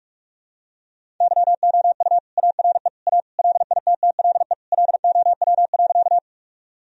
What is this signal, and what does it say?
Morse code, heard as a keyed tone, about 45 wpm, 700 Hz. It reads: YOU ARE A LITTLE LOW.
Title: Sentences from Top 1000 Words - Review Part 1 - 45wpm